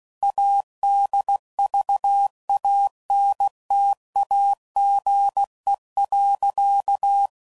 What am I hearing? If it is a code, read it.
Morse code, heard as a keyed tone, about 16 wpm, 800 Hz. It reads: ADVANTAGE.
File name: CW Code Groups @15WPM / Technical Subjects / PwrSupply